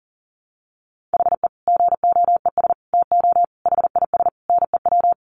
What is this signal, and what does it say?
Morse code, heard as a keyed tone, about 40 wpm, 700 Hz. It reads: HE GOES TO HIS NEW
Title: Easy Phrases 40wpm